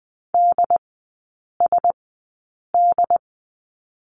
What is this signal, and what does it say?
Morse code, heard as a keyed tone, about 20 wpm, spaced as for 8 wpm, 700 Hz. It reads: DSD
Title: 067 Lesson 12e - Letter d - Three Characters 20wpm